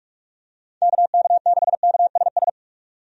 Morse code, heard as a keyed tone, about 46 wpm, 700 Hz. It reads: KK6KSS
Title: USA Call Signs - 6 Characters - 45wpm